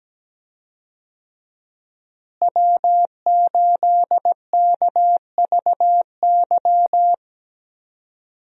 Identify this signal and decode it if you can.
Morse code, heard as a keyed tone, about 17 wpm, 700 Hz. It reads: W8KVY
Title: USA Call Signs - 5 Characters - Repeated 2x - 17wpm